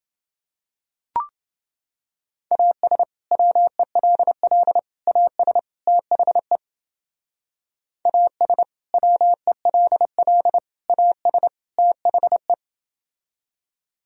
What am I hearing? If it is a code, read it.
Morse code, heard as a keyed tone, about 29 wpm, 700 Hz. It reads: AS WELL AS THE AS WELL AS THE